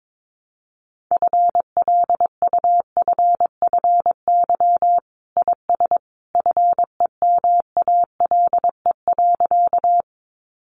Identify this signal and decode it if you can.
Morse code, heard as a keyed tone, about 22 wpm, 700 Hz. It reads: FLUFFY IS FEMALE.
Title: Easy Phrases 22wpm